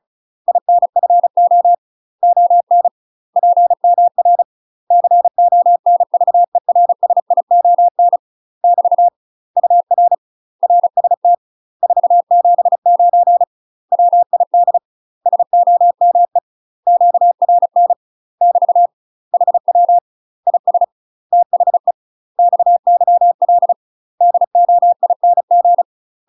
Morse code, heard as a keyed tone, about 33 wpm, 700 Hz. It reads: INFO ON PMR CONVERSION <BT> UR RST 479 WID SOME QRN <BT> HW IS THE XYL DOING
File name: UK MASTER QSOs TRAINING/35WPM